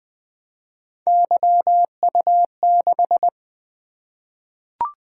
Morse code, heard as a keyed tone, about 20 wpm, 700 Hz. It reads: YU6
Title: Three Letter-Number Combinations - 20wpm